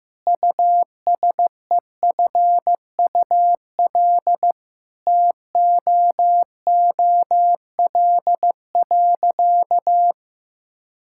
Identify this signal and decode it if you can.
Morse code, heard as a keyed tone, about 16 wpm, 700 Hz. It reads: USEFUL TOOL.
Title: Sentences from Top 1000 Words - Review Part 6 - 15wpm